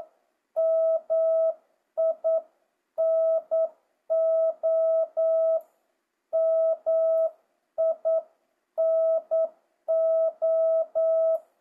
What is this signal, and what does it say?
Morse code, heard as a keyed tone, about 9 wpm, 650 Hz. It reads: MINO MINO
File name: jh8cbh / 2025cw